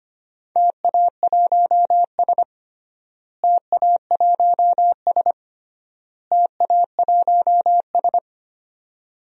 Morse code, heard as a keyed tone, about 25 wpm, 700 Hz. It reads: TA1S TA1S TA1S
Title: DXCC-100 Call Signs - 4 Characters - Repeated 3x - 25wpm